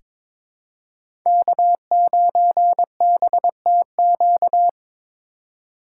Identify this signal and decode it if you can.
Morse code, heard as a keyed tone, about 22 wpm, 700 Hz. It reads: K9BTQ
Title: K1USN SST QSO Practice - 22wpm